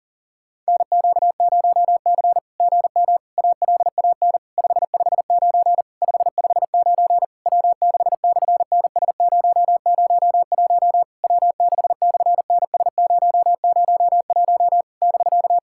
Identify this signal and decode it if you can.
Morse code, heard as a keyed tone, about 40 wpm, 700 Hz. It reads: NQ0C GM ALAN 559 559 W6/NS001 W6/NS001 <BK>
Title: SOTA QSO Practice 40wpm